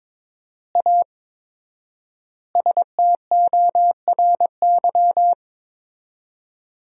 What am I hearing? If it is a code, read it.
Morse code, heard as a keyed tone, about 22 wpm, 700 Hz. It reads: A STORY
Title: 4x Word-Spacing - Sentences from Top 500 Words - 22wpm